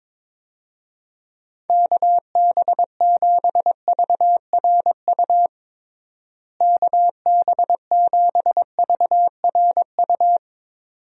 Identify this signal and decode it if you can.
Morse code, heard as a keyed tone, about 22 wpm, 700 Hz. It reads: KB7VRU KB7VRU
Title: USA Call Signs - 6 Characters - Repeated 2x - 22wpm